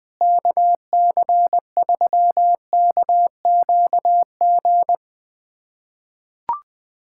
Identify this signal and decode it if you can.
Morse code, heard as a keyed tone, about 20 wpm, 700 Hz. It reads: KC3KQG
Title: USA Callsign DE Callsign - 6 Characters - 20wpm